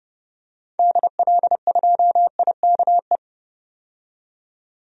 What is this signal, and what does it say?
Morse code, heard as a keyed tone, about 30 wpm, 700 Hz. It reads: DL2IKE
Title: DXCC-100 Call Signs - 6 Characters - 30wpm